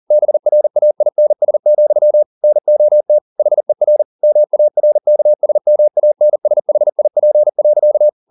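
Morse code, heard as a keyed tone, about 40 wpm, 600 Hz. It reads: BRAINS, NOT HER MARKSMANSHIP.
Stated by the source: Morse code at 40 wpm, practice text, CW med Gnister.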